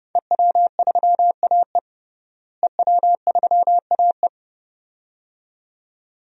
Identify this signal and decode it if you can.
Morse code, heard as a keyed tone, about 30 wpm, 700 Hz. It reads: EW3AE EW3AE
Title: DXCC-100 Call Signs - 5 Characters - Repeated 2x - 30wpm